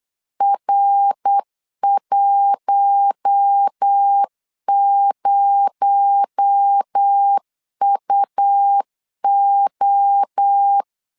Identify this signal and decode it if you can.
Morse code, heard as a keyed tone, about 8 wpm, 800 Hz. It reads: R10UO
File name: LEZIONI / 6 PKX0 / 6 PKX0 8wpm 3341